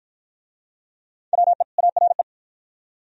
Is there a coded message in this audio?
Yes: Morse code from 1.3 to 2.2 s, heard as a keyed tone, about 52 wpm, 690 Hz. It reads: WE ARE